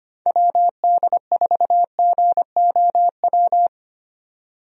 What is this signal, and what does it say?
Morse code, heard as a keyed tone, about 25 wpm, 700 Hz. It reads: WD4GOW
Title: USA Call Signs - 6 Characters - 25wpm